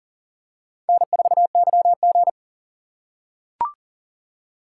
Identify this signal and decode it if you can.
Morse code, heard as a keyed tone, about 40 wpm, 700 Hz. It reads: N4YG
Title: USA Call Signs - 4 Characters - 2x Character Spacing - 40wpm